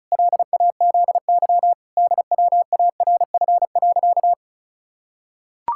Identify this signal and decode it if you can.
Morse code, heard as a keyed tone, about 34 wpm, 700 Hz. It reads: LAZY DWARF.
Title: Pangram Challenge v2 - 35wpm